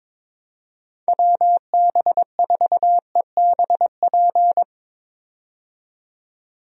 Morse code, heard as a keyed tone, about 22 wpm, 700 Hz. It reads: WB4EBP
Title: USA Call Signs - 6 Characters - 22wpm